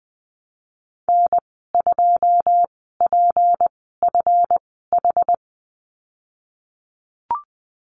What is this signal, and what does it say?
Morse code, heard as a keyed tone, about 20 wpm, spaced as for 15 wpm, 700 Hz. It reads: N2PFH
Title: 193 Lesson 31g - Letter v - Call Signs 20wpm